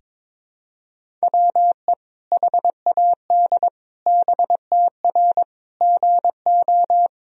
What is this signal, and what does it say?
Morse code, heard as a keyed tone, about 22 wpm, 700 Hz. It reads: WE HAD BTR GO
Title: Sentences from Top 400 Words and Abbreviations 22wpm